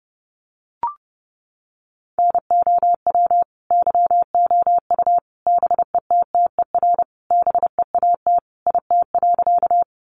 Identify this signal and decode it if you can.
Morse code, heard as a keyed tone, about 30 wpm, 700 Hz. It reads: NOW YOU BETTER BEAT IT.
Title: Sentences from Top 1300 Words 30wpm